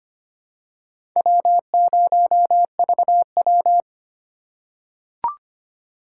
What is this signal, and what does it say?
Morse code, heard as a keyed tone, about 25 wpm, 700 Hz. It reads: W0VW
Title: USA Call Signs - 8x Character Spacing - 25wpm